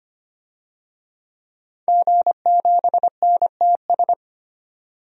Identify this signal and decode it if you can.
Morse code, heard as a keyed tone, about 25 wpm, 700 Hz. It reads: G7NTS
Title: DXCC-100 Call Signs - 5 Characters - 25wpm